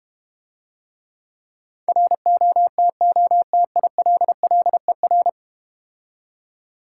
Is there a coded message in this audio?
Yes: Morse code from 1.9 to 5.3 s, heard as a keyed tone, about 32 wpm, 700 Hz. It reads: ROTOTILLER